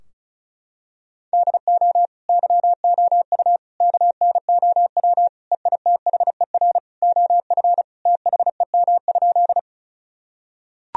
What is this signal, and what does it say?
Morse code, heard as a keyed tone, about 36 wpm, 700 Hz. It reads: DO YOU KNOW EITHER OF THEM?
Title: Sentences from Top 1000 Words - Review Part 4 - 35wpm